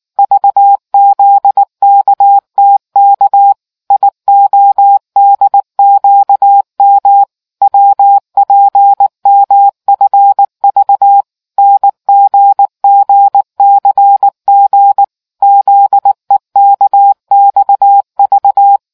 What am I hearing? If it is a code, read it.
Morse code, heard as a keyed tone, about 19 wpm, 800 Hz. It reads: VZKTK IODQM WPMFV NGGCG ZEKXV